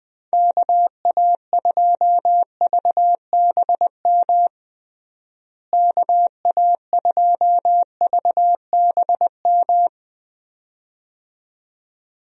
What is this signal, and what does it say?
Morse code, heard as a keyed tone, about 20 wpm, 700 Hz. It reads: KA2VBM KA2VBM
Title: USA Call Signs - 6 Characters - Repeated 2x - 20wpm